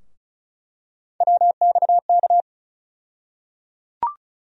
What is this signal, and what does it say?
Morse code, heard as a keyed tone, about 36 wpm, 700 Hz. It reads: WXK
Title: USA Call Sign Postfixes - 3 Letters 35wpm